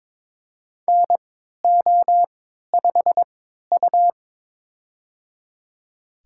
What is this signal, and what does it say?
Morse code, heard as a keyed tone, about 21 wpm, spaced as for 12 wpm, 690 Hz. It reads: NO5U
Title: USA Call Signs - 4 Characters - 3x Character Spacing - 22wpm